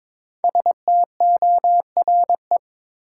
Morse code, heard as a keyed tone, about 22 wpm, 690 Hz. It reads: STORE
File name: 5 Letter Words - First 3 Letters Given - 22wpm